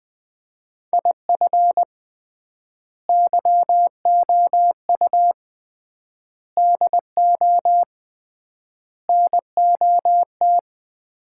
Morse code, heard as a keyed tone, about 20 wpm, 700 Hz. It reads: IF YOU DO NOT
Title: 3x Word-Spacing - Sets of 5 Words - 20wpm